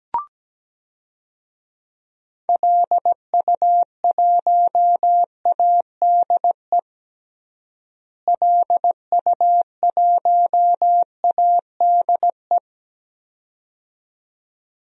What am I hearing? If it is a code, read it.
Morse code, heard as a keyed tone, about 17 wpm, 700 Hz. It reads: LU1ADE LU1ADE
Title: DXCC-100 Call Signs - 6 Characters - Repeated 2x - 17wpm